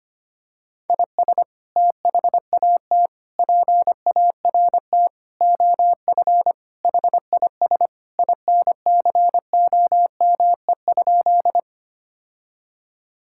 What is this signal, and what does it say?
Morse code, heard as a keyed tone, about 25 wpm, 700 Hz. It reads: IS THAT PART OF HIS INCOME?
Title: Sentences from Top 1000 Words - Review Part 5 - 25wpm